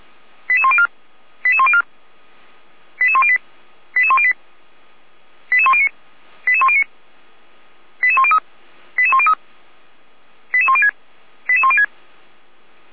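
Five-tone selective call (ZVEI), heard as quick runs of five tones, tones of about 70 ms. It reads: 88104 88104 88108 88108 88109 88109 88103 88103 88106 88106
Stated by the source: ZVEI five-tone call 88192 received by SDR on the 455 channel, feuerwehr-orlamuende.org fire brigade audio server.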